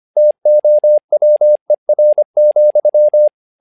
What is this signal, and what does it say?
Morse code, heard as a keyed tone, about 25 wpm, 600 Hz. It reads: TOWER,
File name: learning center / tower trouble / tower trouble 25wpm